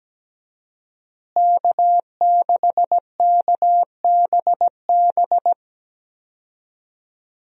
Morse code, heard as a keyed tone, about 17 wpm, 700 Hz. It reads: K6KBB
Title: USA Call Signs - 5 Characters - Repeated 3x - 17wpm